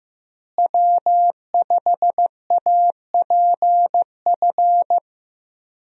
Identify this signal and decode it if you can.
Morse code, heard as a keyed tone, about 16 wpm, 700 Hz. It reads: W5APF
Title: USA Call Signs - 5 Characters - Repeated 3x - 15wpm